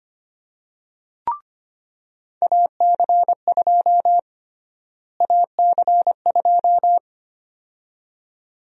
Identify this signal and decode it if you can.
Morse code, heard as a keyed tone, about 25 wpm, 700 Hz. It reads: AC2 AC2
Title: USA Call Sign Prefixes - Repeated 2x - 25wpm